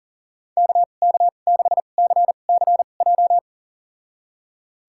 Morse code, heard as a keyed tone, about 40 wpm, spaced as for 29 wpm, 700 Hz. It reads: KK6CCJ
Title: USA Call Signs - 2x Character Spacing - 40wpm